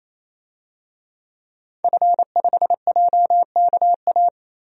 Morse code, heard as a keyed tone, about 28 wpm, 700 Hz. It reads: F5JKA
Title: DXCC-100 Call Signs - 5 Characters - 28wpm